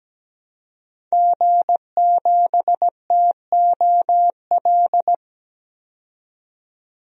Morse code, heard as a keyed tone, about 17 wpm, 700 Hz. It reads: G7TOL
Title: DXCC-100 Call Signs - 5 Characters - 17wpm